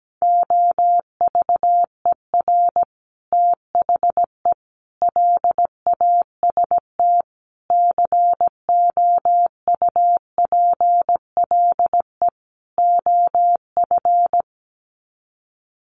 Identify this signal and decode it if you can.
Morse code, heard as a keyed tone, about 17 wpm, 700 Hz. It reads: OVER THE LAST COUPLE OF
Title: Sets of 5 Words Encore 17wpm